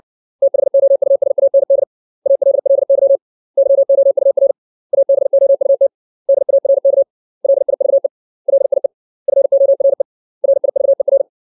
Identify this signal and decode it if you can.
Morse code, heard as a keyed tone, about 60 wpm, 550 Hz. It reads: THOUSAND ALLY YOUR ABOUT DARK LIVE LIE FORE RIVER
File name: 632 words / 632 Words 60WPM